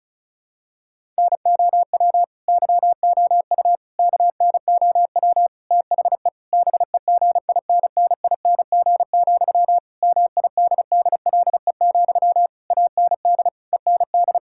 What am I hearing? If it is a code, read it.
Morse code, heard as a keyed tone, about 34 wpm, 700 Hz. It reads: NOW YOU KNOW THE BEGINNING, MIDDLE, AND END
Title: Sentences from Top 1000 Words - Review Part 6 - 35wpm